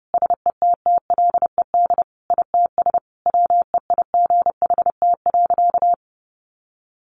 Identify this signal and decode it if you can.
Morse code, heard as a keyed tone, about 30 wpm, 700 Hz. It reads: SETTLED ITS WEIGHT.
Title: A Sound Of Thunder 30wpm